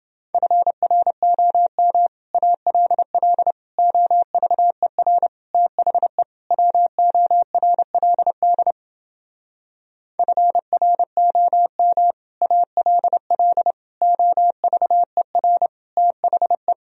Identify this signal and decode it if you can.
Morse code, heard as a keyed tone, about 29 wpm, 700 Hz. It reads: FROM ALL OVER THE WORLD FROM ALL OVER THE